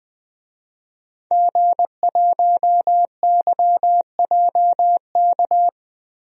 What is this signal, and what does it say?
Morse code, heard as a keyed tone, about 20 wpm, 700 Hz. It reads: G1YJK
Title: DXCC-100 Call Signs - 5 Characters - Repeated 2x - 20wpm